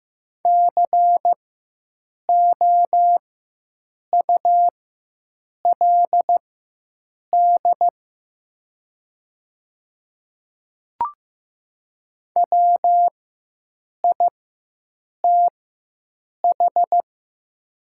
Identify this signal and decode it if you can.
Morse code, heard as a keyed tone, about 15 wpm, spaced as for 7 wpm, 700 Hz. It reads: COULD WITH
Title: 4x Character Spacing - Top 100 Words - 15wpm